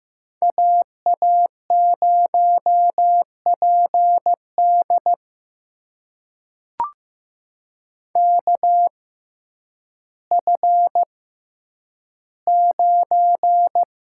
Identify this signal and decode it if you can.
Morse code, heard as a keyed tone, about 15 wpm, spaced as for 9 wpm, 700 Hz. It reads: AA0PD KF9
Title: USA Call Signs - 5 Characters - 6x Character Spacing - 15wpm